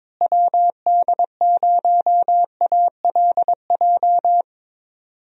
Morse code, heard as a keyed tone, about 22 wpm, 700 Hz. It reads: WD0ALJ
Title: USA Call Signs - 6 Characters - Repeated 3x - 22wpm